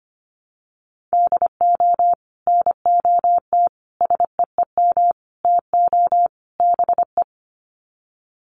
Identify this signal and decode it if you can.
Morse code, heard as a keyed tone, about 25 wpm, 700 Hz. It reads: DO NOT SEEM TO BE
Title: Sets of 5 Words Encore2 25wpm